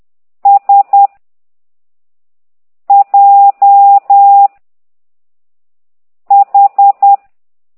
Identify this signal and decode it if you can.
Morse code, heard as a keyed tone, about 10 wpm, 800 Hz. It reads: S J H
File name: ct2idw / exerc6